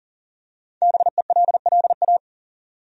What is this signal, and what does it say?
Morse code, heard as a keyed tone, about 37 wpm, 700 Hz. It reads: BELLA
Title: USA Common Names - Max 5 Letters - 40wpm